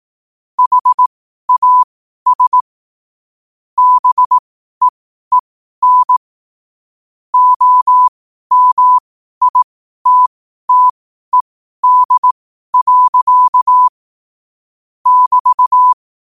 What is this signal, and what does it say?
Morse code, heard as a keyed tone, about 18 wpm, spaced as for 12 wpm, 1000 Hz. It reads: HAS BEEN OMITTED. <BT>
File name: qotdfiles / 12wpm